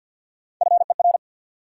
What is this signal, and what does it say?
Morse code, heard as a keyed tone, about 50 wpm, 700 Hz. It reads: FER
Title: Top 90 QSO Elements 50wpm